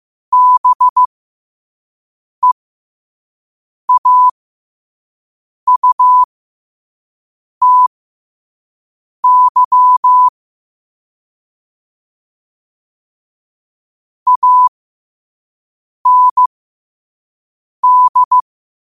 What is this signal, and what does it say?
Morse code, heard as a keyed tone, about 15 wpm, spaced as for 5 wpm, 1000 Hz. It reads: BEAUTY AND